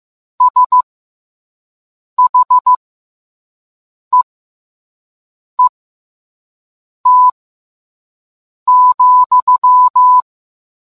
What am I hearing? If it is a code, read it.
Morse code, heard as a keyed tone, about 15 wpm, spaced as for 5 wpm, 1000 Hz. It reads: SHEET,